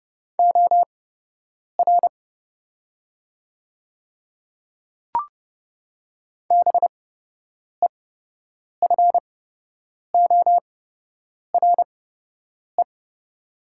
Morse code, heard as a keyed tone, about 30 wpm, spaced as for 8 wpm, 700 Hz. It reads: OR BEFORE
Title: 8x Character Spacing - Top 200 Words - 30wpm